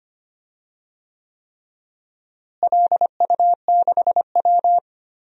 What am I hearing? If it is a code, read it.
Morse code, heard as a keyed tone, about 25 wpm, 700 Hz. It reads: LU6W
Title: DXCC-100 Call Signs - 4 Characters - Repeated 3x - 25wpm